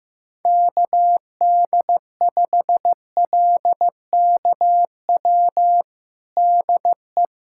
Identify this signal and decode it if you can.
Morse code, heard as a keyed tone, about 15 wpm, 700 Hz. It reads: KD5LKW DE
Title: USA Callsign DE Callsign - 6 Characters - 15wpm